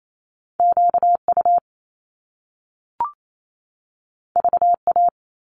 Morse code, heard as a keyed tone, about 28 wpm, 700 Hz. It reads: QU VA
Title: Top Two Letter Combinations in Words 28wpm